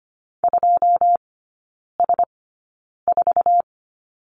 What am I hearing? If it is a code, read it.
Morse code, heard as a keyed tone, about 25 wpm, spaced as for 9 wpm, 700 Hz. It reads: 2S4
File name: Lesson 17e - Letter u - Three Characters 25wpm